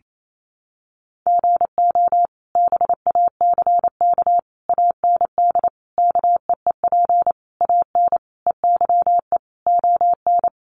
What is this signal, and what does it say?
Morse code, heard as a keyed tone, about 28 wpm, 700 Hz. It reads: GO BACK AND KEEP AN EYE ON
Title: Sentences from Top 300 Words 28wpm